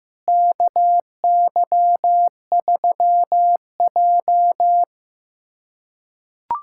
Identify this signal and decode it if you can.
Morse code, heard as a keyed tone, about 16 wpm, 700 Hz. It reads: KY3J
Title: USA Callsign DE Callsign - 4 Characters - 15wpm